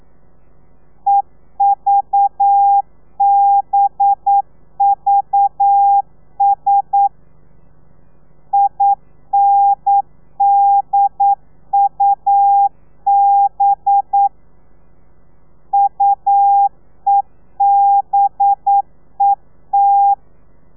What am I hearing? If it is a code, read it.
Morse code, heard as a keyed tone, about 9 wpm, 800 Hz. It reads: EVBVS INDUB UEBET
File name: telegrafia / i2tfj / Lez4 40